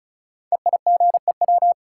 Morse code, heard as a keyed tone, about 36 wpm, 700 Hz. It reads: EIGEW
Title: Code Groups - 5 Letters - 35wpm